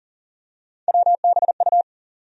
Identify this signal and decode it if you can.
Morse code, heard as a keyed tone, about 40 wpm, 700 Hz. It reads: WBU
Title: USA Call Sign Postfixes - 3 Letters 40wpm